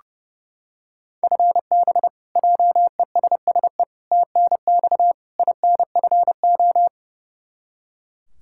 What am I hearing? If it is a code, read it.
Morse code, heard as a keyed tone, about 30 wpm, 700 Hz. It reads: FB JESSE TNX INFO
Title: Practice QSO - Exchange 3 - More Info - 30wpm